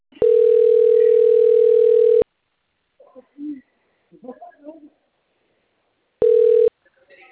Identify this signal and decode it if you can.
Ringback tone (North America), 440 + 480 Hz, with speech.
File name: out- - - - - .78015